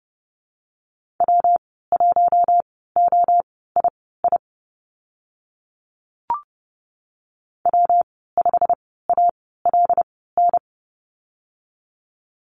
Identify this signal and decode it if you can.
Morse code, heard as a keyed tone, about 30 wpm, spaced as for 17 wpm, 700 Hz. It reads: W1OII W5ALN
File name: Lesson 19g - Letter w - Call Signs 30wpm